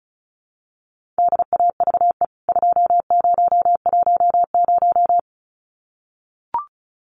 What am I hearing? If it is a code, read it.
Morse code, heard as a keyed tone, about 34 wpm, 700 Hz. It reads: DAVE 2010
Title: CWT Exchange 35wpm